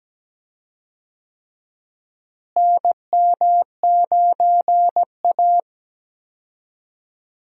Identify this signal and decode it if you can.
Morse code, heard as a keyed tone, about 17 wpm, 700 Hz. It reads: NM9A
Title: USA Call Signs - 4 Characters - Repeated 3x - 17wpm